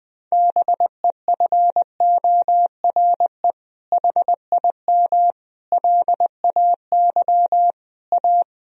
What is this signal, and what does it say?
Morse code, heard as a keyed tone, about 20 wpm, 700 Hz. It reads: BEFORE HIM LAY A
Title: Sentences from Top 1000 Words - Review Part 2 - 20wpm